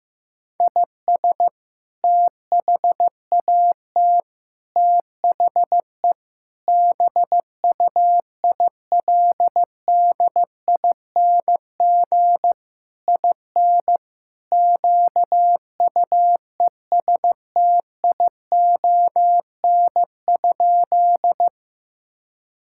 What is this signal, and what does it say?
Morse code, heard as a keyed tone, about 15 wpm, 700 Hz. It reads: IS THAT THE BUILDING IN QUESTION?
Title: Sentences from Top 1000 Words - Review Part 5 - 15wpm